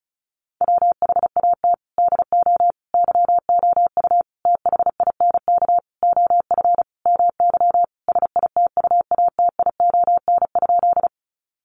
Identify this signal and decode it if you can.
Morse code, heard as a keyed tone, about 34 wpm, 700 Hz. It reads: WHAT DO YOU THINK OF MY SITUATION?
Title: Sentences from Top 1200 Words 35wpm